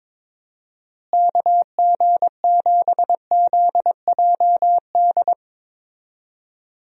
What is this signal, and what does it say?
Morse code, heard as a keyed tone, about 22 wpm, 700 Hz. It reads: KG7ZJD
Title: USA Call Signs - 6 Characters - Repeated 3x - 22wpm